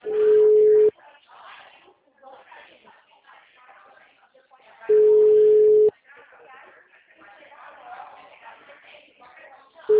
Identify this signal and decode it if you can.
Ringback tone (Europe, 425 Hz), with speech.